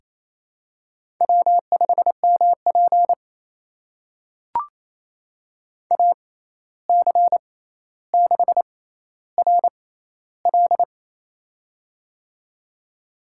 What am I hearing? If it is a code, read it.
Morse code, heard as a keyed tone, about 27 wpm, spaced as for 13 wpm, 700 Hz. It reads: W5MP AC6RL
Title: USA Call Signs - 6x Character Spacing - 28wpm